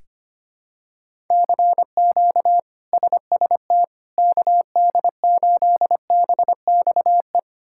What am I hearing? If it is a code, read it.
Morse code, heard as a keyed tone, about 25 wpm, 700 Hz. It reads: CQ SST KD8BXE
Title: K1USN SST QSO Practice - 25wpm